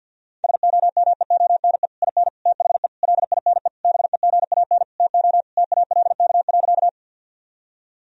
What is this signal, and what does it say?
Morse code, heard as a keyed tone, about 52 wpm, 700 Hz. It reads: SOMEONE IN THE LINE BEGAN TO TALK.
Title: Sentences from Top 1000 Words - Review Part 1 - 50wpm